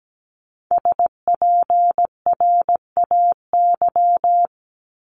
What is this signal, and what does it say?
Morse code, heard as a keyed tone, about 17 wpm, 700 Hz. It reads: SPRAY
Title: Multi-letter Phonemes 17wpm